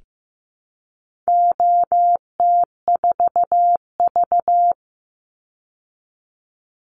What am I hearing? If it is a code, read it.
Morse code, heard as a keyed tone, about 15 wpm, 700 Hz. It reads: OT4V
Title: Call Signs Repeated 3x - 15wpm